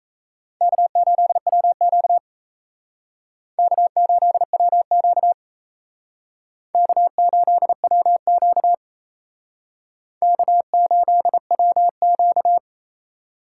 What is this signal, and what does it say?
Morse code, heard as a keyed tone, about 36 wpm, 700 Hz. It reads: K8WQ K8WQ K8WQ K8WQ